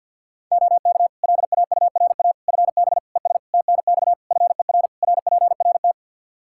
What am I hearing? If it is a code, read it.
Morse code, heard as a keyed tone, about 52 wpm, 700 Hz. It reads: OK LAURA FB ES TNX FER RPRT
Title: K1USN SST QSO Practice 50wpm